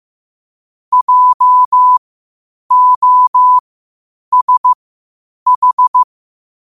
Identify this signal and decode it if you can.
Morse code, heard as a keyed tone, about 15 wpm, spaced as for 8 wpm, 1000 Hz. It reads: JOSH
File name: qotdfiles / 8wpm